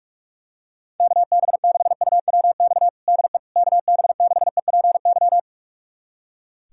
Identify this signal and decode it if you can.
Morse code, heard as a keyed tone, about 46 wpm, 700 Hz. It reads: KB6UWX DE KB6EPY
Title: USA Callsign DE Callsign 45wpm